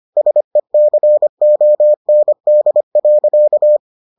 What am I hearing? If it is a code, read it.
Morse code, heard as a keyed tone, about 25 wpm, 600 Hz. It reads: SECOND.